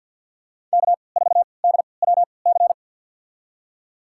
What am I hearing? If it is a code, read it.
Morse code, heard as a keyed tone, about 52 wpm, spaced as for 29 wpm, 700 Hz. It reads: K4DWC
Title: USA Call Signs - 5 Characters - 3x Character Spacing - 50wpm